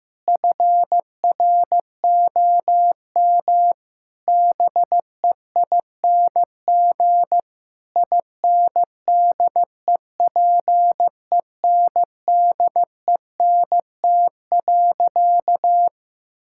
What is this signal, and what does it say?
Morse code, heard as a keyed tone, about 15 wpm, 700 Hz. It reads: FROM BEING INDEPENDENT.
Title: Sentences from Top 1000 Words - Review Part 6 - 15wpm